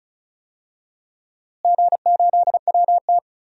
Morse code, heard as a keyed tone, about 36 wpm, 700 Hz. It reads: G8WT
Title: DXCC-100 Call Signs - 4 Characters - Repeated 2x - 35wpm